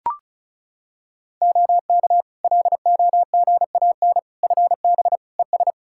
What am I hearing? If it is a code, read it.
Morse code, heard as a keyed tone, about 36 wpm, 700 Hz. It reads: OK LOGAN FB ES
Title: Practice QSO - Exchange 2 - Info - 35wpm